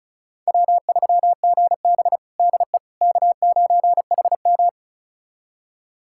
Morse code, heard as a keyed tone, about 36 wpm, 700 Hz. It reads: W3GB DE K9HM
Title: USA Callsign DE Callsign - 4 Characters - 35wpm